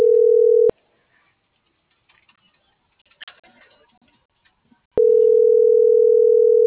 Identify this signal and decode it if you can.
Ringback tone (North America), 440 + 480 Hz.